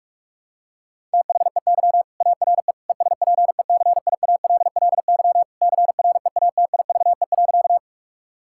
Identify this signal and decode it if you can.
Morse code, heard as a keyed tone, about 46 wpm, 700 Hz. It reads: THEY ARE ESPECIALLY CREATIVE.
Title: Sentences from Top 1000 Words - Review Part 5 - 45wpm